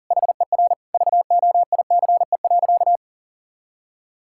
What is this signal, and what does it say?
Morse code, heard as a keyed tone, about 37 wpm, 700 Hz. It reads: HER VOICE.
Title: Sentences from Top 1600 Words 40wpm